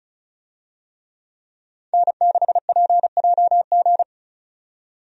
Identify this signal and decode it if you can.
Morse code, heard as a keyed tone, about 36 wpm, 700 Hz. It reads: N6PJG
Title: USA Call Signs - 5 Characters - 35wpm